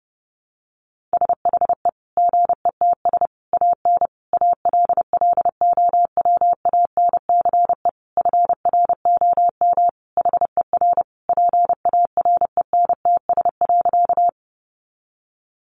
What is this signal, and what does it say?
Morse code, heard as a keyed tone, about 30 wpm, 700 Hz. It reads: SHE GETS AN ALLOWANCE FROM HER PARENTS.
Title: Easy Phrases 30wpm